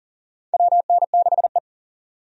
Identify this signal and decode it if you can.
Morse code, heard as a keyed tone, about 40 wpm, 690 Hz. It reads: WN6E
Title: USA Call Signs - 4 Characters - 40wpm